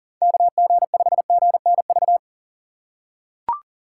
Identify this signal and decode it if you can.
Morse code, heard as a keyed tone, about 40 wpm, 700 Hz. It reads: KG5GNV
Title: USA Call Signs - 6 Characters - 2x Character Spacing - 40wpm